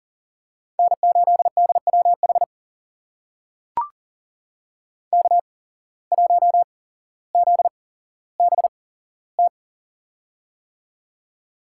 Morse code, heard as a keyed tone, about 39 wpm, spaced as for 17 wpm, 700 Hz. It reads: N8DWH K1ZBT
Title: USA Call Signs - 5 Characters - 8x Character Spacing - 40wpm